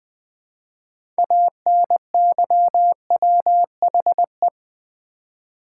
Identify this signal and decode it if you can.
Morse code, heard as a keyed tone, about 20 wpm, 700 Hz. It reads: ANYWHE
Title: Build A Word - 3 Letters At A Time - 8-Letter Words - 20wpm